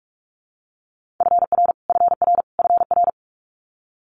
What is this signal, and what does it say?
Morse code, heard as a keyed tone, about 46 wpm, 700 Hz. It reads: FR FR FR
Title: Multi-letter Phonemes Repeated 3x 45wpm